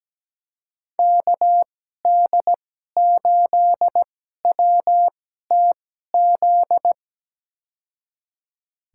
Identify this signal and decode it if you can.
Morse code, heard as a keyed tone, about 17 wpm, spaced as for 12 wpm, 700 Hz. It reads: KD8WTZ
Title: USA Call Signs - 2x Character Spacing - 17wpm